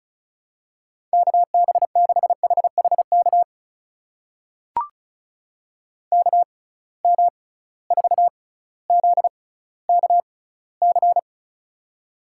Morse code, heard as a keyed tone, about 36 wpm, spaced as for 18 wpm, 690 Hz. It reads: KB6HHK KM4ZKC
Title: USA Call Signs - 6 Characters - 6x Character Spacing - 35wpm